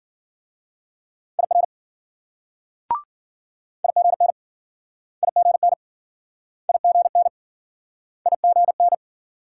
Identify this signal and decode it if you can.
Morse code, heard as a keyed tone, about 53 wpm, 700 Hz. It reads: IR IGN IGN IGN IGN